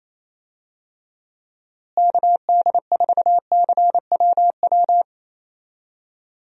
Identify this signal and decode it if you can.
Morse code, heard as a keyed tone, about 28 wpm, 700 Hz. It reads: KD4CWW
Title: USA Call Signs - 6 Characters - Repeated 3x - 28wpm